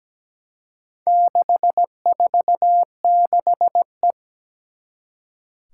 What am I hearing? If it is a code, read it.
Morse code, heard as a keyed tone, about 17 wpm, 700 Hz. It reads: 646E
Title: Four Challenging Letter-Number Combinations 17wpm